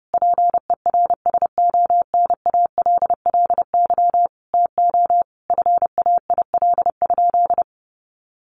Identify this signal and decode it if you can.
Morse code, heard as a keyed tone, about 30 wpm, 700 Hz. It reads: PERSONALLY TO FAIL?
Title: Sentences from Top 1400 Words 30wpm